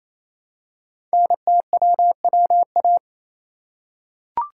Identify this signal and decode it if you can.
Morse code, heard as a keyed tone, about 28 wpm, 700 Hz. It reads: NTWWA
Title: Code Groups - 5 Letters - 28wpm